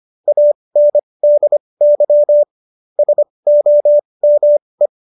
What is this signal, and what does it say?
Morse code, heard as a keyed tone, about 25 wpm, spaced as for 20 wpm, 590 Hz. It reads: ANDY SOME